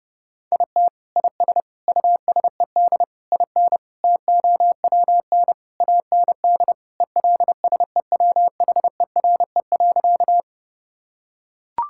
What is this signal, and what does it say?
Morse code, heard as a keyed tone, about 30 wpm, 700 Hz. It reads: IT IS USED IN TOWN AND ELSEWHERE.
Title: Sentences from Top 1600 Words 30wpm